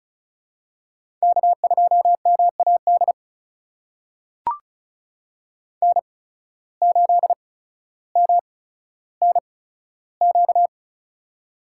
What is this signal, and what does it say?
Morse code, heard as a keyed tone, about 36 wpm, spaced as for 15 wpm, 700 Hz. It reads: K2MAD N8MNQ
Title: USA Call Signs - 5 Characters - 8x Character Spacing - 35wpm